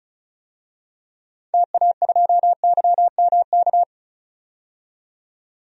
Morse code, heard as a keyed tone, about 34 wpm, 700 Hz. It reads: TA2YMK
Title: DXCC-100 Call Signs - 6 Characters - 35wpm